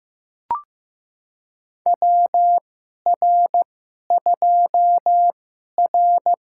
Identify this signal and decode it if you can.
Morse code, heard as a keyed tone, about 15 wpm, spaced as for 11 wpm, 700 Hz. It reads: WR2R
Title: USA Call Signs - 4 Characters - 2x Character Spacing - 15wpm